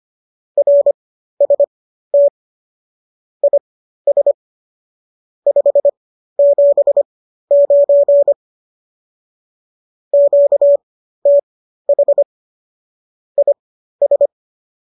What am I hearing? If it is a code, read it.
Morse code, heard as a keyed tone, about 25 wpm, spaced as for 13 wpm, 570 Hz. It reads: RST IS 579 QTH IS